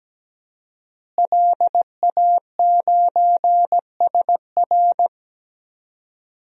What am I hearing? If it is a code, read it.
Morse code, heard as a keyed tone, about 17 wpm, 700 Hz. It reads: LA9SR
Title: DXCC-100 Call Signs - 5 Characters - Repeated 3x - 17wpm